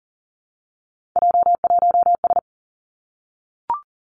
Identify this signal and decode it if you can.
Morse code, heard as a keyed tone, about 38 wpm, 690 Hz. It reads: J1S
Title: Three Challenging Letter-Number Combinations 40wpm